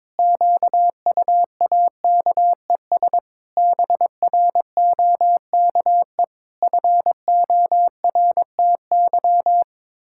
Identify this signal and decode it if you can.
Morse code, heard as a keyed tone, about 22 wpm, 700 Hz. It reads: QUAKES BROKE FORTY
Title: Pangram Challenge v2 - 22wpm